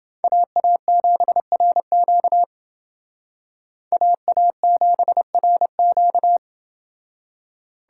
Morse code, heard as a keyed tone, about 28 wpm, 700 Hz. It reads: AA7RQ AA7RQ